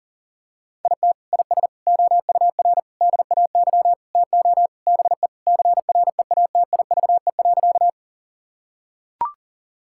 Morse code, heard as a keyed tone, about 37 wpm, 700 Hz. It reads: IT IS OUR DAY TO BE CREATIVE.
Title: Sentences from Top 1000 Words - Review Part 5 - 40wpm